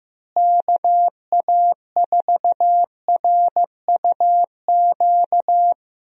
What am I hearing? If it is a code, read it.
Morse code, heard as a keyed tone, about 15 wpm, 700 Hz. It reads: KA4RUQ
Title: USA Call Signs - 6 Characters - Repeated 2x - 15wpm